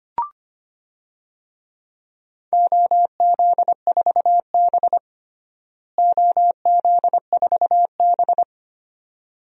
Morse code, heard as a keyed tone, about 25 wpm, 700 Hz. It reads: OZ4B OZ4B
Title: DXCC-100 Call Signs - 4 Characters - Repeated 2x - 25wpm